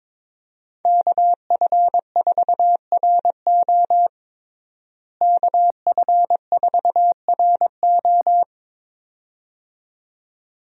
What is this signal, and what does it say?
Morse code, heard as a keyed tone, about 22 wpm, 700 Hz. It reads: KF4RO KF4RO
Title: USA Call Signs - 5 Characters - Repeated 2x - 22wpm